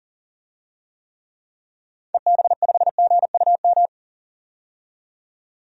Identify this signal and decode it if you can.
Morse code, heard as a keyed tone, about 40 wpm, 700 Hz. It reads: EB5GUM
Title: DXCC-100 Call Signs - 6 Characters - 40wpm